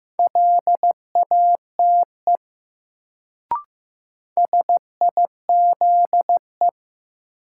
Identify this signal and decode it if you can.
Morse code, heard as a keyed tone, about 15 wpm, 700 Hz. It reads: LATE SIZE
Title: Sets of 3 Words - New Word Drill - 15wpm